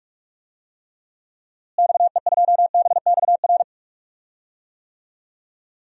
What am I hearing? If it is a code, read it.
Morse code, heard as a keyed tone, about 46 wpm, 700 Hz. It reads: XE2BXR
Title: DXCC-100 Call Signs - 6 Characters - Repeated 3x - 45wpm